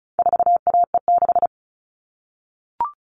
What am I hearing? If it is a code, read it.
Morse code, heard as a keyed tone, about 34 wpm, 700 Hz. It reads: 4AE6
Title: All Cut Numbers - 4-Digit 35wpm